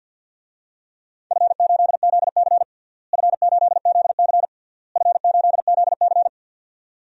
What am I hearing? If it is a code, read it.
Morse code, heard as a keyed tone, about 50 wpm, 700 Hz. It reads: F8ZC F8ZC F8ZC
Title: DXCC-100 Call Signs - 4 Characters - Repeated 3x - 50wpm